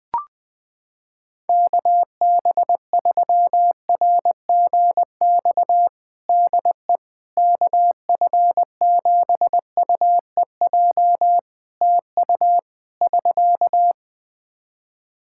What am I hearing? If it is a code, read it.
Morse code, heard as a keyed tone, about 20 wpm, 700 Hz. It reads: KB3RGX DE KF7UEJ TU <SK>
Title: Practice QSO - Exchange 4 - Ending - 20wpm